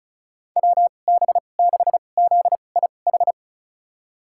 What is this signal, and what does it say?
Morse code, heard as a keyed tone, about 36 wpm, spaced as for 25 wpm, 700 Hz. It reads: WB6ZIH
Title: USA Call Signs - 6 Characters - 2x Character Spacing - 35wpm